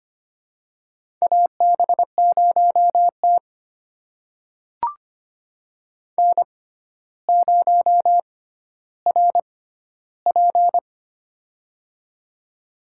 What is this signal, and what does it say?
Morse code, heard as a keyed tone, about 25 wpm, spaced as for 13 wpm, 700 Hz. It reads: AB0T N0RP
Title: USA Call Signs - 4 Characters - 6x Character Spacing - 25wpm